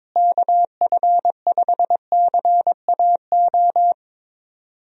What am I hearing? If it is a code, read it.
Morse code, heard as a keyed tone, about 22 wpm, 700 Hz. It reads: KF5CAO
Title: USA Call Signs - 6 Characters - Repeated 3x - 22wpm